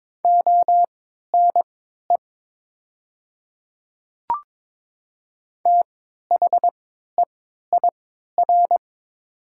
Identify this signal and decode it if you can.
Morse code, heard as a keyed tone, about 22 wpm, spaced as for 13 wpm, 700 Hz. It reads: ONE THEIR
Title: 3x Character Spacing - Top 100 Words - 22wpm